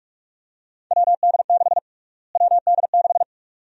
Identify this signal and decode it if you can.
Morse code, heard as a keyed tone, about 45 wpm, 700 Hz. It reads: WD6 WD6
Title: USA Call Sign Prefixes - Repeated 2x - 45wpm